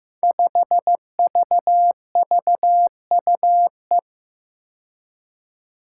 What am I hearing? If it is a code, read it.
Morse code, heard as a keyed tone, about 15 wpm, 700 Hz. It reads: 5VVUE
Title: Five Challenging Letter-Number Combinations 15wpm